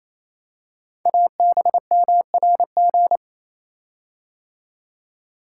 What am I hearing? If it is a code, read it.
Morse code, heard as a keyed tone, about 27 wpm, 700 Hz. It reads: ABMRG
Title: Code Groups - 5 Letters - 28wpm